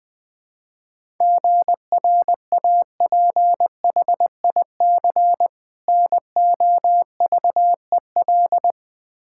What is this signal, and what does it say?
Morse code, heard as a keyed tone, about 21 wpm, 700 Hz. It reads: GRAPHIC NOVEL
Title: Difficult Words with Hints - 20wpm